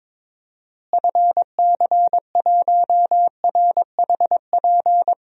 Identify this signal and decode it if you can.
Morse code, heard as a keyed tone, about 22 wpm, 700 Hz. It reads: FC1RHP
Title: DXCC-100 Call Signs - 6 Characters - Repeated 3x - 22wpm